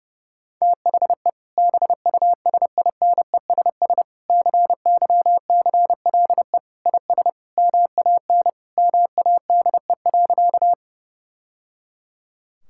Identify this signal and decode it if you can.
Morse code, heard as a keyed tone, about 30 wpm, 700 Hz. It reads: THE BUSINESS CYCLE IS MAN MADE.
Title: Sentences from Top 1000 Words - Review Part 6 - 30wpm